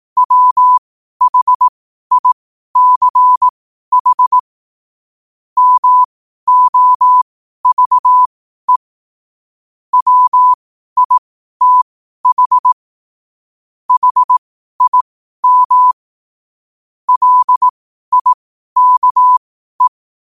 Morse code, heard as a keyed tone, about 18 wpm, spaced as for 12 wpm, 1000 Hz. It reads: WHICH MOVE WITH HIM LIKE
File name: qotdfiles / 12wpm